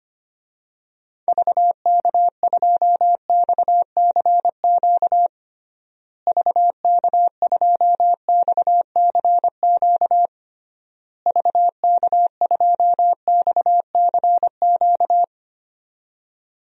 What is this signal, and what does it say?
Morse code, heard as a keyed tone, about 25 wpm, 700 Hz. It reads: VK2XCQ VK2XCQ VK2XCQ
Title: DXCC-100 Call Signs - 6 Characters - Repeated 3x - 25wpm